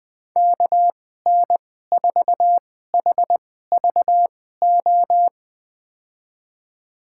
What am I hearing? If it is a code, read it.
Morse code, heard as a keyed tone, about 20 wpm, spaced as for 15 wpm, 700 Hz. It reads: KN4HVO
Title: USA Call Signs - 6 Characters - 2x Character Spacing - 20wpm